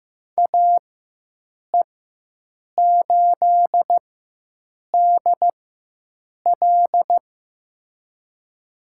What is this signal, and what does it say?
Morse code, heard as a keyed tone, about 15 wpm, spaced as for 7 wpm, 700 Hz. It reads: AE8DL
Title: USA Call Signs - 4x Character Spacing - 15wpm